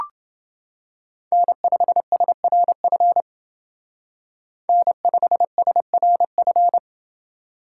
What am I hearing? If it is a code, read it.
Morse code, heard as a keyed tone, about 28 wpm, 700 Hz. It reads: N5SRF N5SRF